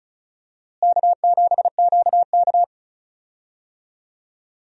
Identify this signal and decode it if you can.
Morse code, heard as a keyed tone, about 34 wpm, 700 Hz. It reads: K7QK
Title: USA Call Signs - 4 Characters - Repeated 2x - 35wpm